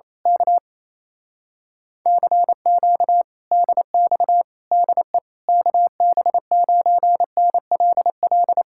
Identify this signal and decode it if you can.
Morse code, heard as a keyed tone, about 27 wpm, 700 Hz. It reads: K CQ DX DE KB9NLL
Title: Speed-Racing Common QSO Phrases - 28wpm